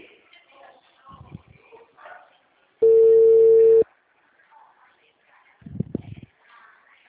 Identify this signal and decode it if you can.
Ringback tone (China, 450 Hz), with speech.